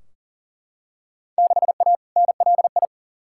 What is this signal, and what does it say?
Morse code, heard as a keyed tone, about 40 wpm, 700 Hz. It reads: 6A NLI
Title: Field Day Exchanges Encore - 40wpm